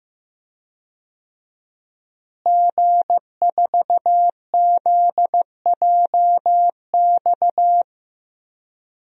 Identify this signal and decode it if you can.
Morse code, heard as a keyed tone, about 15 wpm, 700 Hz. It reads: G4ZJX
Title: DXCC-100 Call Signs - 5 Characters - Repeated 3x - 15wpm